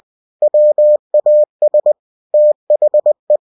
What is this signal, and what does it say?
Morse code, heard as a keyed tone, about 20 wpm, 600 Hz. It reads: WAS THE
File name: cw-books / winnie-the-pooh / winnie-20wpm-page-0053